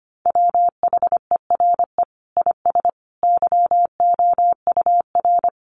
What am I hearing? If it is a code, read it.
Morse code, heard as a keyed tone, about 25 wpm, 700 Hz. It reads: WHERE IS YOUR
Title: Sentences from Top 1400 Words 25wpm